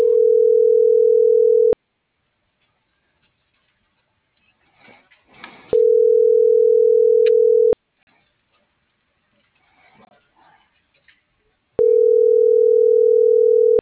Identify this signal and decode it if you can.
Ringback tone (North America), 440 + 480 Hz.